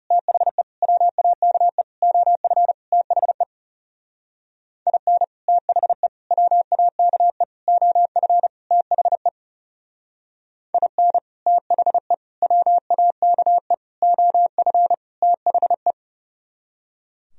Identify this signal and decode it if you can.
Morse code, heard as a keyed tone, about 38 wpm, 700 Hz. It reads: THE WAKE OF THE IN THE WAKE OF THE IN THE WAKE OF THE